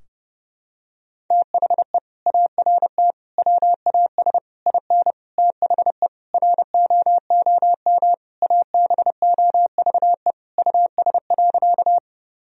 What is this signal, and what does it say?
Morse code, heard as a keyed tone, about 30 wpm, 700 Hz. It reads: THE ART WAS IN THE ROOM ABOVE US.
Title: Sentences from Top 1000 Words - Review Part 3 - 30wpm